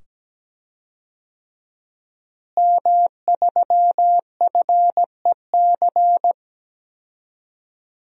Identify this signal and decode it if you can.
Morse code, heard as a keyed tone, about 17 wpm, 700 Hz. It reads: M3FEC
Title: DXCC-100 Call Signs - 5 Characters - Repeated 2x - 17wpm